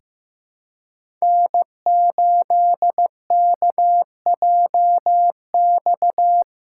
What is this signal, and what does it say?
Morse code, heard as a keyed tone, about 15 wpm, 700 Hz. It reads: N8KJX
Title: USA Callsign DE Callsign - 5 Characters - 15wpm